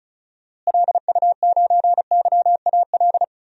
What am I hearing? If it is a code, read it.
Morse code, heard as a keyed tone, about 36 wpm, 700 Hz. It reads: LU9YAL
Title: DXCC-100 Call Signs - 6 Characters - Repeated 3x - 35wpm